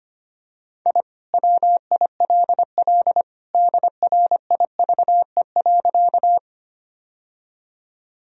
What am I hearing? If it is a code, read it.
Morse code, heard as a keyed tone, about 25 wpm, 700 Hz. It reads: I WILL DRIVE.
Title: Sentences from Top 1000 Words - Review Part 2 - 25wpm